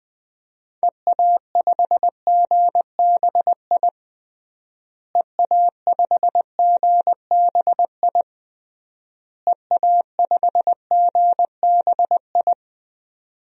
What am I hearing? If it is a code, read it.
Morse code, heard as a keyed tone, about 19 wpm, 700 Hz. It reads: EA5GBI EA5GBI EA5GBI
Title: DXCC-100 Call Signs - 6 Characters - Repeated 3x - 20wpm